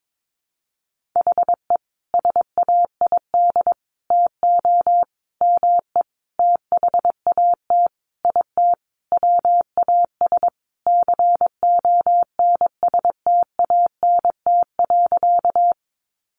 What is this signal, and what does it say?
Morse code, heard as a keyed tone, about 22 wpm, 700 Hz. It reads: HE SAID TO ME THAT IT WAS CONSTANT.
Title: Sentences from Top 1500 Words 22wpm